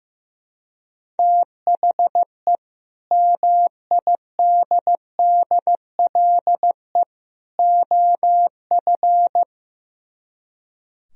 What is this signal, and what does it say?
Morse code, heard as a keyed tone, about 17 wpm, 700 Hz. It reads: THE MIDDLE OF
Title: Speed-Racing - Sets of 3 Words 15wpm